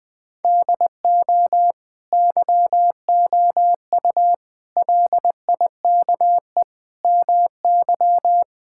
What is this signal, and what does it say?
Morse code, heard as a keyed tone, about 20 wpm, 700 Hz. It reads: DO YOU LIKE MY
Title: Sentences from Top 1000 Words - Review Part 5 - 20wpm